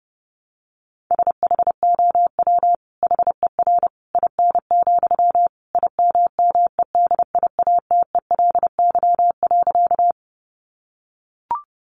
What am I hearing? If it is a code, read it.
Morse code, heard as a keyed tone, about 30 wpm, 700 Hz. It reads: SHOW HER IN, IMMEDIATELY.
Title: Sentences from Top 1100 Words 30wpm